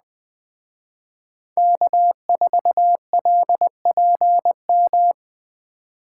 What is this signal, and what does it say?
Morse code, heard as a keyed tone, about 20 wpm, 700 Hz. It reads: K4LPM
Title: Speed Racing - USA 1x3 Call Signs 17wpm